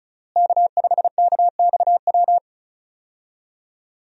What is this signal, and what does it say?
Morse code, heard as a keyed tone, about 36 wpm, 700 Hz. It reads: K5KXW
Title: USA Call Signs - 5 Characters - Repeated 3x - 35wpm